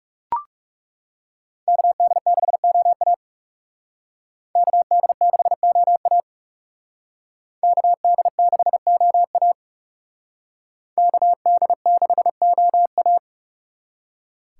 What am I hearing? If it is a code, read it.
Morse code, heard as a keyed tone, about 36 wpm, 700 Hz. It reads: KD6OA KD6OA KD6OA KD6OA
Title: Speed Racing - USA 2x2 Call Signs 30wpm